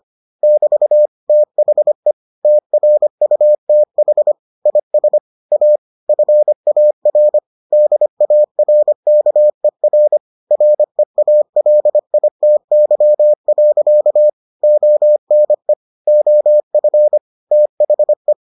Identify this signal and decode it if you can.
Morse code, heard as a keyed tone, about 25 wpm, 600 Hz. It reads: <BT> THE TRUTH IS A FAR DARKER REALITY. ONE OF THE